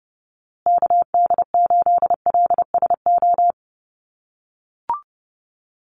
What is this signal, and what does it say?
Morse code, heard as a keyed tone, about 30 wpm, 700 Hz. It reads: KD8LSO
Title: USA Call Signs Repeated 2x 30wpm